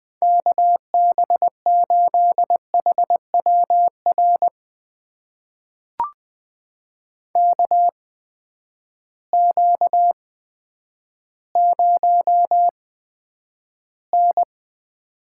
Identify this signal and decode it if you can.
Morse code, heard as a keyed tone, about 20 wpm, spaced as for 10 wpm, 700 Hz. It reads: KB8HWR KQ0N
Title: USA Call Signs - 8x Character Spacing - 20wpm